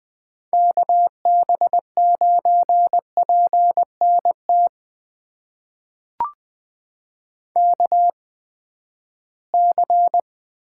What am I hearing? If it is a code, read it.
Morse code, heard as a keyed tone, about 20 wpm, spaced as for 14 wpm, 700 Hz. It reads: KB9PNT KC
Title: USA Call Signs - 6 Characters - 8x Character Spacing - 20wpm